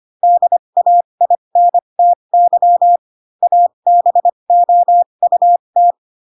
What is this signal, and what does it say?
Morse code, heard as a keyed tone, about 25 wpm, spaced as for 22 wpm, 700 Hz. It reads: DAINTY ABOUT
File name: Pinocchio / Pinocchio ch7 18 25 ik1hge